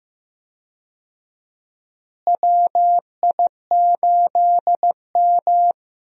Morse code, heard as a keyed tone, about 15 wpm, 700 Hz. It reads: WI8M
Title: USA Call Signs - 4 Characters - Repeated 2x - 15wpm